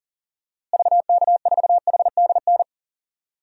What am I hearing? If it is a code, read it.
Morse code, heard as a keyed tone, about 40 wpm, 700 Hz. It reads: VK4HDN
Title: DXCC-100 Call Signs - 6 Characters - Repeated 3x - 40wpm